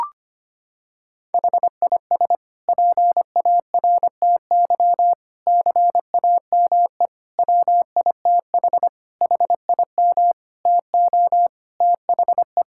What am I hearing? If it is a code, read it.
Morse code, heard as a keyed tone, about 25 wpm, 700 Hz. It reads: HIS PARTY CAME WITH HIM TO THE
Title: Sentences from Top 1600 Words 25wpm